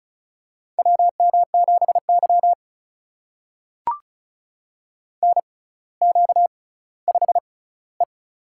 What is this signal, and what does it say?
Morse code, heard as a keyed tone, about 36 wpm, spaced as for 18 wpm, 700 Hz. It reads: WM7Y NQ5E
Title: USA Call Signs - 4 Characters - 6x Character Spacing - 35wpm